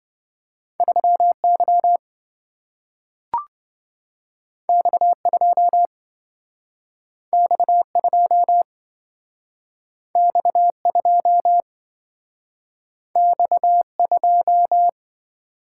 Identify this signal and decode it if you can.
Morse code, heard as a keyed tone, about 27 wpm, 700 Hz. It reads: 3Y X2 X2 X2 X2